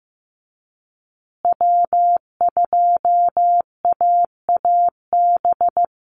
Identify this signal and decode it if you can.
Morse code, heard as a keyed tone, about 15 wpm, 700 Hz. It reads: W2AAB
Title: Call Signs Repeated 2x - 15wpm